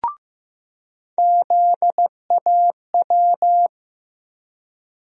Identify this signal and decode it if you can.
Morse code, heard as a keyed tone, about 16 wpm, 700 Hz. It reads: ZAW
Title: USA Call Sign Postfixes - 3 Letters 15wpm